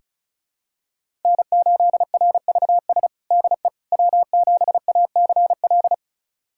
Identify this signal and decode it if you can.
Morse code, heard as a keyed tone, about 34 wpm, 700 Hz. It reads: N8RVS DE W7ACL
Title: USA Callsign DE Callsign - 5 Characters - 35wpm